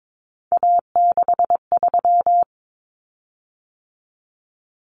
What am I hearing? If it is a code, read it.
Morse code, heard as a keyed tone, about 22 wpm, 700 Hz. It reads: A63
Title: Common Cut Numbers - 3-Digit 22wpm